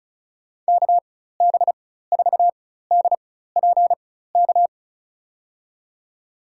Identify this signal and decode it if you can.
Morse code, heard as a keyed tone, about 36 wpm, spaced as for 16 wpm, 700 Hz. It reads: KB4DPK
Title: USA Call Signs - 6 Characters - 4x Character Spacing - 35wpm